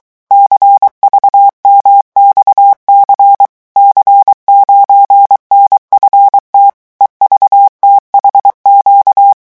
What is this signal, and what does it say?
Morse code, heard as a keyed tone, about 23 wpm, 800 Hz. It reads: CVMXC C9NFT EVTHQ
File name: morse mp3 / revision13-vitesse 22 mots